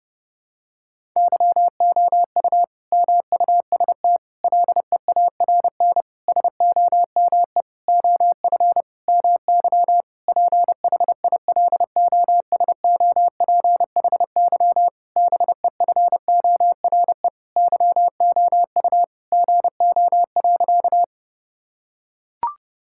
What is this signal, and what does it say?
Morse code, heard as a keyed tone, about 32 wpm, 700 Hz. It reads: YOU MUST LEARN SOME OF MY PHILOSOPHY BEFORE YOU GO.
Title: Sentences from Top 1000 Words - Review Part 6 - 30wpm